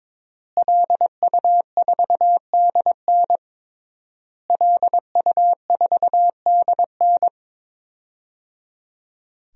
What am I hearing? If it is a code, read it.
Morse code, heard as a keyed tone, about 23 wpm, 700 Hz. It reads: LU4DN LU4DN
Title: DXCC-100 Call Signs - 5 Characters - Repeated 3x - 22wpm